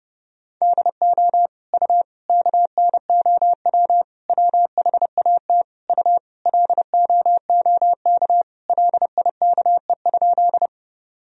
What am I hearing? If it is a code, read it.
Morse code, heard as a keyed tone, about 32 wpm, 700 Hz. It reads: DO U KNOW WHAT U LOOK LIKE?
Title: Sentences from Top 100 Words and Abbreviations 30wpm